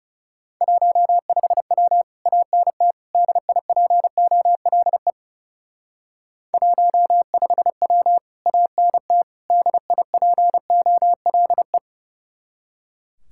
Morse code, heard as a keyed tone, about 36 wpm, 700 Hz. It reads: 15W ANT DIPOLE 15W ANT DIPOLE
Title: Speed-Racing Common QSO Phrases Encore - 30wpm